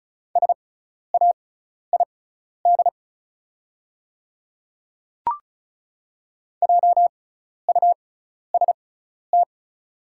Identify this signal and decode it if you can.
Morse code, heard as a keyed tone, about 36 wpm, spaced as for 12 wpm, 700 Hz. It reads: SAID JUST
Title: 6x Character Spacing - Top 200 Words - 35wpm